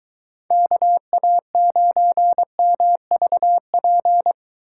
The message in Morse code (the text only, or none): KA9MVP